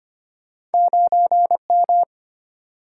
Morse code, heard as a keyed tone, about 34 wpm, 700 Hz. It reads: 9M